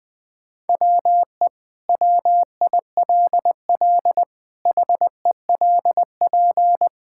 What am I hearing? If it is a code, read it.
Morse code, heard as a keyed tone, about 20 wpm, 700 Hz. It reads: WE WILL HELP
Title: Sentences with 2-Word Phrases - Spoken Beginning - 20wpm